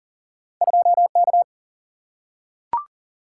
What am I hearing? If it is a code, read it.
Morse code, heard as a keyed tone, about 40 wpm, 700 Hz. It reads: 2K
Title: Single-Letter Single-Number 40wpm